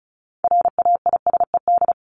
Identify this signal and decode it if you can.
Morse code, heard as a keyed tone, about 34 wpm, 700 Hz. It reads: RAISED
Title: ED Suffix - Top 100 Words with Base Words 35wpm